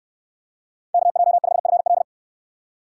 Morse code, heard as a keyed tone, about 69 wpm, 700 Hz. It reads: KJ5FL